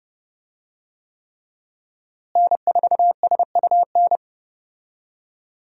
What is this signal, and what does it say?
Morse code, heard as a keyed tone, about 30 wpm, 700 Hz. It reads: N4SUN
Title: USA Call Signs - 5 Characters - Repeated 3x - 30wpm